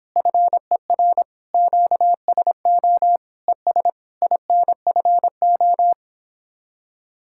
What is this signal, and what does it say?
Morse code, heard as a keyed tone, about 29 wpm, 700 Hz. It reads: FER QSO ES INFO